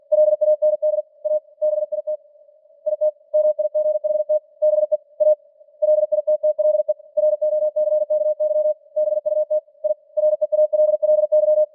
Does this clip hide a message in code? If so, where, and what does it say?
Yes: Morse code from 0.1 to 11.6 s, heard as a keyed tone, about 44 wpm, 610 Hz. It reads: LAND A BIT IT MIGHT BE A LITTLE ROCKY BUT I REALLY